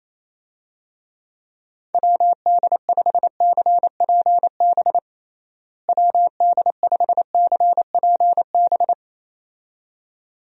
Morse code, heard as a keyed tone, about 28 wpm, 700 Hz. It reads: WD5CPB WD5CPB
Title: USA Call Signs - 6 Characters - Repeated 2x - 28wpm